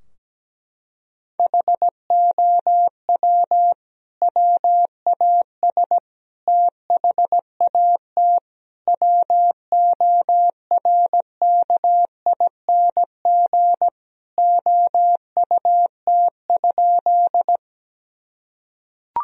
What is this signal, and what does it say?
Morse code, heard as a keyed tone, about 17 wpm, 700 Hz. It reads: HOW WAS THAT WORKING OUT?
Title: Sentences from Top 1000 Words - Review Part 4 - 17wpm